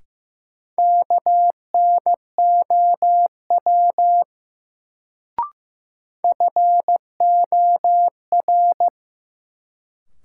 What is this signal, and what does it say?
Morse code, heard as a keyed tone, about 15 wpm, 700 Hz. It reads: KNOW FOR
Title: Sentences from Top 100 Words - New Word Drill - 15wpm